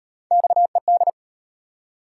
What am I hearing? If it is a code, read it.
Morse code, heard as a keyed tone, about 38 wpm, 700 Hz. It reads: XED